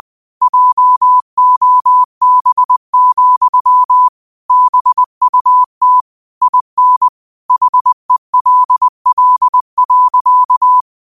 Morse code, heard as a keyed tone, about 20 wpm, 1000 Hz. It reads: JOB, BUT IN HELL.
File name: qotdfiles / 20wpm